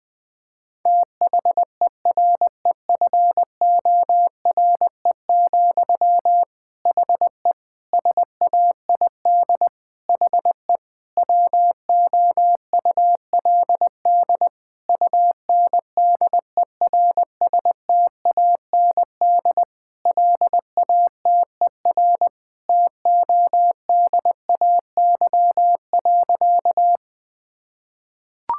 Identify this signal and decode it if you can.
Morse code, heard as a keyed tone, about 20 wpm, 700 Hz. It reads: THEREFORE, HE SAID HE WOULD UNDERSTAND LATER TODAY.
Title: Sentences from Top 1000 Words - Review Part 5 - 20wpm